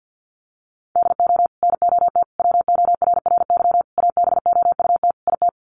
Morse code, heard as a keyed tone, about 50 wpm, 700 Hz. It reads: DO NOT WORRY ABOUT IT
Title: Sets of 5 Words 50wpm